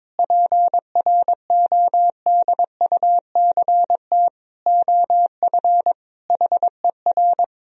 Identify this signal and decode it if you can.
Morse code, heard as a keyed tone, about 22 wpm, 700 Hz. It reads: PRODUCT OF HER
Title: Sentences from Top 1000 Words - Review Part 2 - 22wpm